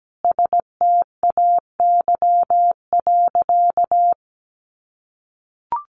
Morse code, heard as a keyed tone, about 17 wpm, 700 Hz. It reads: STAY.
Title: Sentences from Top 500 Words 17wpm